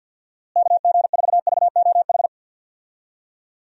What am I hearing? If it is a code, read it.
Morse code, heard as a keyed tone, about 52 wpm, 700 Hz. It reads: KG4VOH
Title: USA Call Signs - 6 Characters - Repeated 3x - 50wpm